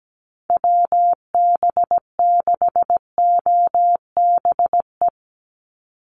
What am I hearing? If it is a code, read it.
Morse code, heard as a keyed tone, about 17 wpm, 700 Hz. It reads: WB6OBE
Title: USA Call Signs 17wpm